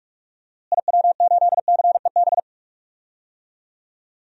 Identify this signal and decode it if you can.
Morse code, heard as a keyed tone, about 46 wpm, 700 Hz. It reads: IW8CEB
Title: DXCC-100 Call Signs - 6 Characters - Repeated 2x - 45wpm